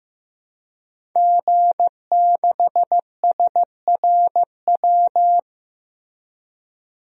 Morse code, heard as a keyed tone, about 15 wpm, 700 Hz. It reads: G6SRW
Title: DXCC-100 Call Signs - 5 Characters - Repeated 2x - 15wpm